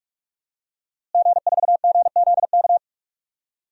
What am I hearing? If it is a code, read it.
Morse code, heard as a keyed tone, about 46 wpm, 700 Hz. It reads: G4GZK
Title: DXCC-100 Call Signs - 5 Characters - Repeated 2x - 45wpm